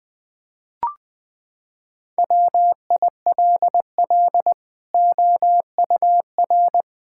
Morse code, heard as a keyed tone, about 20 wpm, 700 Hz. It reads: WILL OUR
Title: Sentences from Top 1000 Words - Review Part 5 - 20wpm